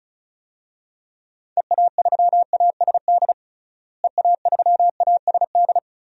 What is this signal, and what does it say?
Morse code, heard as a keyed tone, about 36 wpm, 700 Hz. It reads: EA3ASD EA3ASD
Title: DXCC-100 Call Signs - 6 Characters - Repeated 3x - 35wpm